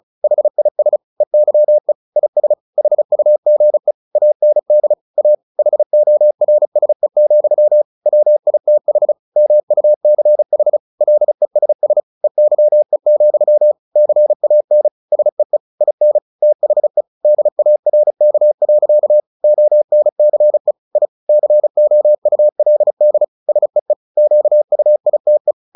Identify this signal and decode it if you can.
Morse code, heard as a keyed tone, about 34 wpm, 600 Hz. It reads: HIS EYE IS HUGE AND A HORSE, WITH MUCH LESS EYE, CAN SEE IN THE DARK. ONCE I COULD SEE QUITE
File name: the old man and the sea / the old man and the sea 1 35wpm